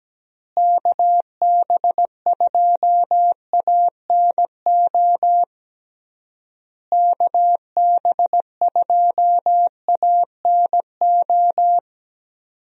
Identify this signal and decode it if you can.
Morse code, heard as a keyed tone, about 17 wpm, 700 Hz. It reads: KB2ANO KB2ANO
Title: USA Call Signs - 6 Characters - Repeated 3x - 17wpm